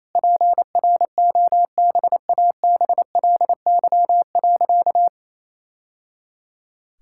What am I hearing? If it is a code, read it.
Morse code, heard as a keyed tone, about 28 wpm, 700 Hz. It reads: PROBABLY.
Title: Sentences from Top 1000 Words - Review Part 5 - 28wpm